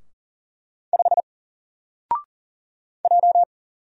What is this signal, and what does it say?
Morse code, heard as a keyed tone, about 40 wpm, 700 Hz. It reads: 5 J
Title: Single Letter-Number Alt 40wpm